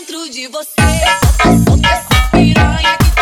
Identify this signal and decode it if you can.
There is no code.